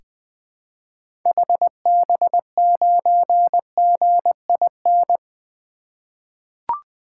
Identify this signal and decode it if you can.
Morse code, heard as a keyed tone, about 21 wpm, 700 Hz. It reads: HB9GIN
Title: 2023 SOTA Activator Call Signs - 20wpm